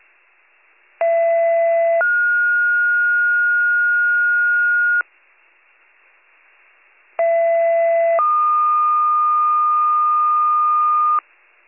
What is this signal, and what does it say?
Two-tone page, heard as a one-second tone, then a three-second tone, about 670 then 1430 Hz, about 670 then 1220 Hz. It is heard from 1.0 to 11.2 s.